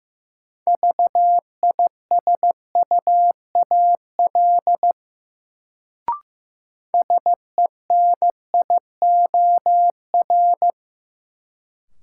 Morse code, heard as a keyed tone, about 15 wpm, 700 Hz. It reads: VISUAL SENIOR
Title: Sets of 5 Words Encore - New Word Drill - 15wpm